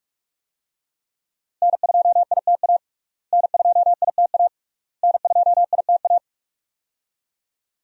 Morse code, heard as a keyed tone, about 46 wpm, 700 Hz. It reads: N2ITA N2ITA N2ITA
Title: USA Call Signs - 5 Characters - Repeated 3x - 45wpm